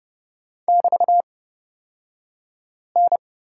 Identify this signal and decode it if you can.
Morse code, heard as a keyed tone, about 30 wpm, spaced as for 5 wpm, 700 Hz. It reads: <BT>N
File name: Lesson 36d - BT - Two Characters 30wpm